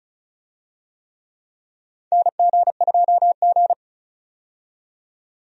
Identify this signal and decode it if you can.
Morse code, heard as a keyed tone, about 34 wpm, 700 Hz. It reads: NG2G
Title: USA Call Signs - 4 Characters - 35wpm